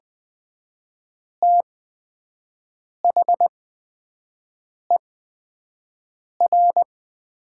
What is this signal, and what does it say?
Morse code, heard as a keyed tone, about 20 wpm, spaced as for 5 wpm, 700 Hz. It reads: THER
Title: 8x Character Spacing - Top 200 Words - 20wpm